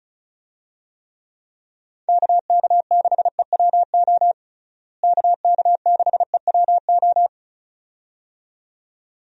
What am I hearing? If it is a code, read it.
Morse code, heard as a keyed tone, about 34 wpm, 700 Hz. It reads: KK6EWO KK6EWO
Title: USA Call Signs - 6 Characters - Repeated 2x - 35wpm